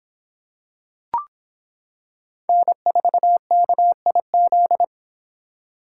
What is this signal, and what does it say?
Morse code, heard as a keyed tone, about 26 wpm, 700 Hz. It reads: N4KIZ